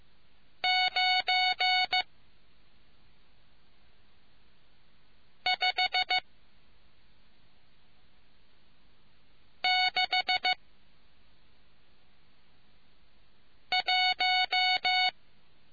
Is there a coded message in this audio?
Yes: Morse code from 0.6 to 15.1 s, heard as a keyed tone, about 15 wpm, 740 Hz. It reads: 9 5 6 1